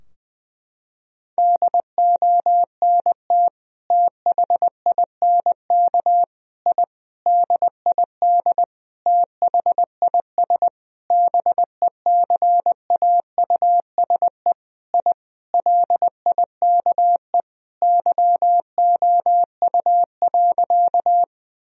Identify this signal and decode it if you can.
Morse code, heard as a keyed tone, about 20 wpm, 700 Hz. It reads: DONT THINK I DID THIS BECAUSE I LIKE YOU.
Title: Sentences from Top 1000 Words - Review Part 3 - 20wpm